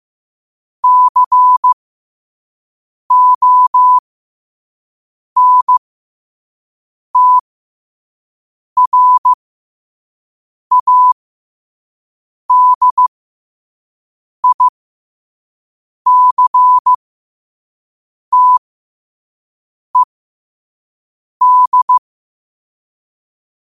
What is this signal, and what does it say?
Morse code, heard as a keyed tone, about 15 wpm, spaced as for 5 wpm, 1000 Hz. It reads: CONTRADICTED
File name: qotdfiles / 5wpm